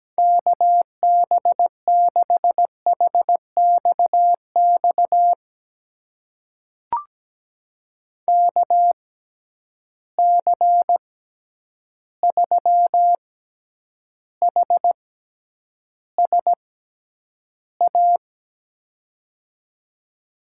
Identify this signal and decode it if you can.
Morse code, heard as a keyed tone, about 17 wpm, spaced as for 9 wpm, 700 Hz. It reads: KB6HXX KC3HSA
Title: USA Call Signs - 6 Characters - 6x Character Spacing - 17wpm